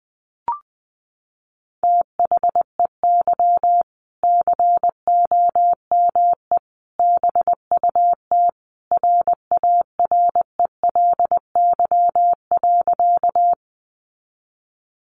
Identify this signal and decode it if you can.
Morse code, heard as a keyed tone, about 20 wpm, 700 Hz. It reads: THEY COME BUT RARELY.
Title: Sentences from Top 1400 Words 20wpm